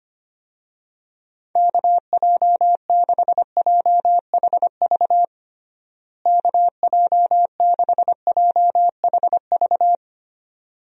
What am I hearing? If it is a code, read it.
Morse code, heard as a keyed tone, about 25 wpm, 700 Hz. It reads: KJ6JHV KJ6JHV
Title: USA Call Signs - 6 Characters - Repeated 2x - 25wpm